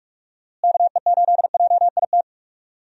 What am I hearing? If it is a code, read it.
Morse code, heard as a keyed tone, about 46 wpm, 700 Hz. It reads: KE8JIT
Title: USA Call Signs - 6 Characters - Repeated 2x - 45wpm